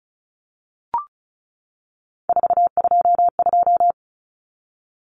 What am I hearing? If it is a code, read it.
Morse code, heard as a keyed tone, about 34 wpm, 700 Hz. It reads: 422
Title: Numbers - 3-Digit 35wpm